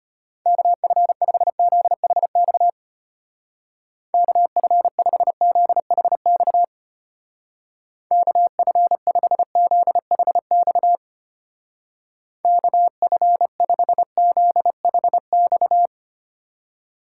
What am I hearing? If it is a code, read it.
Morse code, heard as a keyed tone, about 31 wpm, 700 Hz. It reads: KF5ZHX KF5ZHX KF5ZHX KF5ZHX